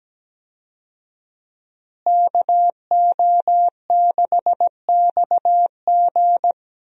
Morse code, heard as a keyed tone, about 17 wpm, 700 Hz. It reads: KO6XG
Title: USA Call Signs - 5 Characters - Repeated 3x - 17wpm